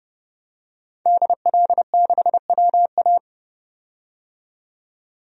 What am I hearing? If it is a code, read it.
Morse code, heard as a keyed tone, about 29 wpm, 700 Hz. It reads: DL6WA
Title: DXCC-100 Call Signs - 5 Characters - 30wpm